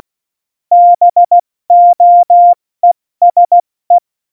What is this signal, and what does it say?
Morse code, heard as a keyed tone, about 16 wpm, 700 Hz. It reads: BOESE
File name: Morsekurs / Witze / Witz 17.Dezember 2019, 70 F80